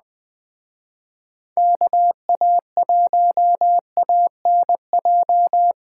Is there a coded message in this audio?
Yes: Morse code from 1.6 to 5.7 s, heard as a keyed tone, about 20 wpm, 700 Hz. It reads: KA1ANJ